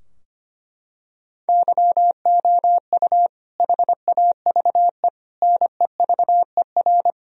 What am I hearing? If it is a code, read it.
Morse code, heard as a keyed tone, about 25 wpm, 700 Hz. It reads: YOU HAVE NEVER
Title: Sentences from Top 1000 Words - Review Part 2 - 25wpm